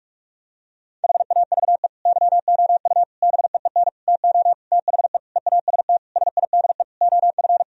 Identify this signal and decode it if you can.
Morse code, heard as a keyed tone, about 46 wpm, 700 Hz. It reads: HAVE YOU BEEN TO THE EAST SIDE OF
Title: Sentences from Top 1000 Words - Review Part 3 - 45wpm